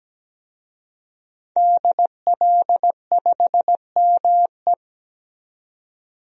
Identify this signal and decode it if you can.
Morse code, heard as a keyed tone, about 17 wpm, 700 Hz. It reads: DL5ME
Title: DXCC-100 Call Signs - 5 Characters - 17wpm